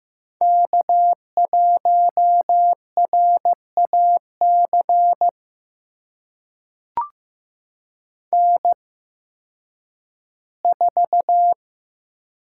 Morse code, heard as a keyed tone, about 15 wpm, spaced as for 10 wpm, 700 Hz. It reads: K1RAC N4
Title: USA Call Signs - 5 Characters - 8x Character Spacing - 15wpm